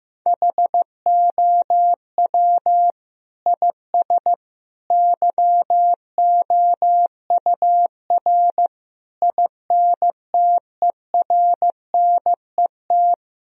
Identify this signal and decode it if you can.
Morse code, heard as a keyed tone, about 15 wpm, 700 Hz. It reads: HOW IS YOUR INTERNET
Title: Sentences from Top 1000 Words - Review Part 4 - 15wpm